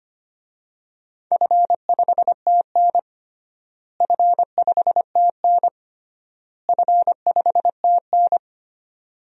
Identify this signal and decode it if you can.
Morse code, heard as a keyed tone, about 25 wpm, 700 Hz. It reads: F5TN F5TN F5TN
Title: DXCC-100 Call Signs - 4 Characters - Repeated 3x - 25wpm